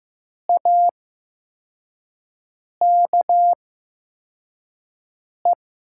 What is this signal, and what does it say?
Morse code, heard as a keyed tone, about 15 wpm, spaced as for 4 wpm, 700 Hz. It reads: AKE
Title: 8x Character Spacing - Top 200 Words - 15wpm